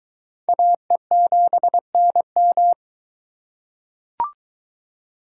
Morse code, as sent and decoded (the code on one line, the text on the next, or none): .- . --... -. --
AE7NM